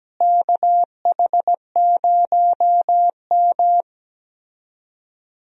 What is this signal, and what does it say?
Morse code, heard as a keyed tone, about 17 wpm, 700 Hz. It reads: KH0M
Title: USA Call Signs - 4 Characters - Repeated 2x - 17wpm